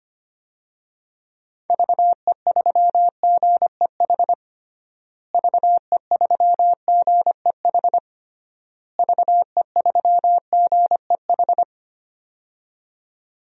Morse code, heard as a keyed tone, about 25 wpm, 700 Hz. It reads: VE3GEH VE3GEH VE3GEH
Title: DXCC-100 Call Signs - 6 Characters - Repeated 3x - 25wpm